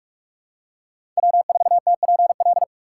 Morse code, heard as a keyed tone, about 41 wpm, 690 Hz. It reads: W4TPL
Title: USA Call Signs - 5 Characters - 6x Character Spacing - 45wpm